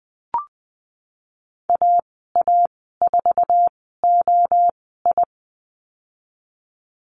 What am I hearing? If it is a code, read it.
Morse code, heard as a keyed tone, about 20 wpm, spaced as for 15 wpm, 700 Hz. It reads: AA4OI
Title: 055 Lesson 10g - Letter r - Call Signs 20wpm